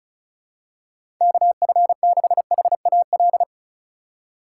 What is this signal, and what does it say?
Morse code, heard as a keyed tone, about 36 wpm, 700 Hz. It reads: KF6HAL
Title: USA Call Signs - 6 Characters - 4x Character Spacing - 35wpm